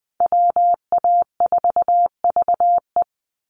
Morse code, heard as a keyed tone, about 20 wpm, 700 Hz. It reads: WA4VE
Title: USA Call Signs 20wpm